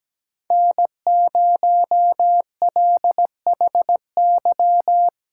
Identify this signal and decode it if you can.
Morse code, heard as a keyed tone, about 17 wpm, 700 Hz. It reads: N0LHY
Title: USA Callsign DE Callsign - 5 Characters - 17wpm